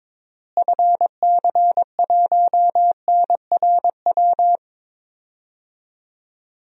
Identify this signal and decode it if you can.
Morse code, heard as a keyed tone, about 22 wpm, 700 Hz. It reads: FC1NRW
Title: DXCC-100 Call Signs - 6 Characters - Repeated 3x - 22wpm